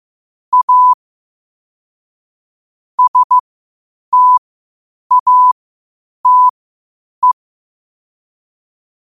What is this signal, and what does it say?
Morse code, heard as a keyed tone, about 15 wpm, spaced as for 8 wpm, 1000 Hz. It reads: A STATE
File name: qotdfiles / 8wpm